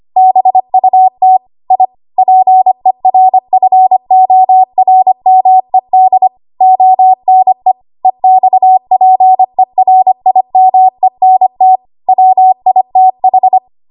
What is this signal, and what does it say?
Morse code, heard as a keyed tone, about 25 wpm, 750 Hz. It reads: BUT I PERFORMED ONE EXPERIMENT WITH
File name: Archive / 25 WPM / 25WPM